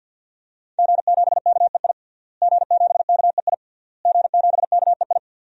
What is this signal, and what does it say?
Morse code, heard as a keyed tone, about 51 wpm, 700 Hz. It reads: G7KEI G7KEI G7KEI
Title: DXCC-100 Call Signs - 5 Characters - Repeated 3x - 50wpm